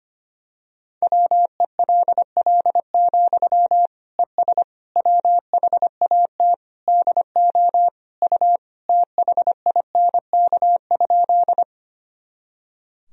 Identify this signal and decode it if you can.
Morse code, heard as a keyed tone, about 25 wpm, 700 Hz. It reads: WELL, ES WHAT DO U THINK?
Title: Sentences from Top 200 Words and Abbreviations 25wpm